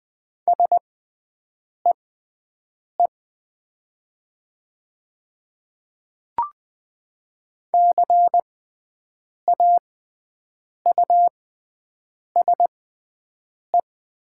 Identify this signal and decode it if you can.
Morse code, heard as a keyed tone, about 20 wpm, spaced as for 7 wpm, 700 Hz. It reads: SEE CAUSE
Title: 6x Character Spacing - Top 200 Words - 20wpm